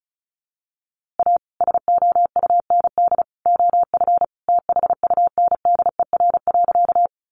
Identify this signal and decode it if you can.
Morse code, heard as a keyed tone, about 34 wpm, 700 Hz. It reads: A SOUND OF THUNDER.
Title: A Sound Of Thunder 35wpm